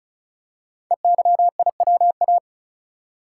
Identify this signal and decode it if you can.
Morse code, heard as a keyed tone, about 36 wpm, 700 Hz. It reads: EYIWA
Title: Code Groups - 5 Letters - 35wpm